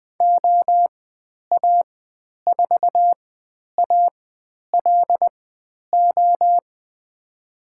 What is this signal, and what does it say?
Morse code, heard as a keyed tone, about 20 wpm, spaced as for 10 wpm, 700 Hz. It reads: OA4ALO
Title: 095 Lesson 16g - Letter c - Call Signs 20wpm